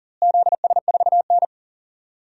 Morse code, heard as a keyed tone, about 40 wpm, 700 Hz. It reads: ZS4N
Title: DXCC-100 Call Signs - 4 Characters - Repeated 3x - 40wpm